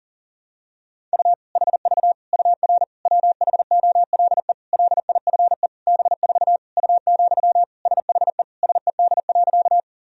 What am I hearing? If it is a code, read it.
Morse code, heard as a keyed tone, about 40 wpm, 700 Hz. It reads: U HV UR WHOLE LIFE B4 U, SHE SED.
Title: Sentences from Top 400 Words and Abbreviations 40wpm